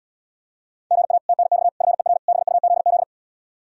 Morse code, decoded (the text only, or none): GA TNX FER BUZZ